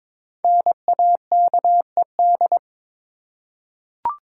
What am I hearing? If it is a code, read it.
Morse code, heard as a keyed tone, about 22 wpm, 700 Hz. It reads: NAKED
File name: 5 Letter Words - First 3 Letters Given - 22wpm